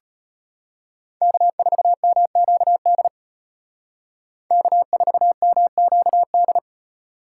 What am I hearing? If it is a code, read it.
Morse code, heard as a keyed tone, about 31 wpm, 700 Hz. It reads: K4MQD K4MQD
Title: Speed Racing - USA 1x3 Call Signs 25wpm